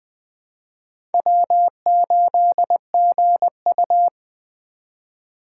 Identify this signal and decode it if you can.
Morse code, heard as a keyed tone, about 21 wpm, 700 Hz. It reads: W8GU
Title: K1USN SST QSO Practice - 20wpm